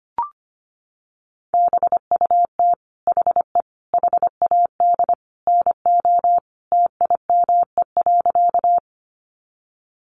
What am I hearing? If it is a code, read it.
Morse code, heard as a keyed tone, about 25 wpm, 700 Hz. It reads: BUT HE HAD NO TIME.
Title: Sentences from Top 100 Words 25wpm